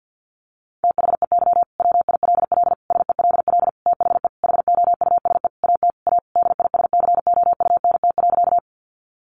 Morse code, heard as a keyed tone, about 50 wpm, 700 Hz. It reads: THEY WILL SELL THE HOUSE AT A DISCOUNT.
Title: Sentences from Top 1200 Words 50wpm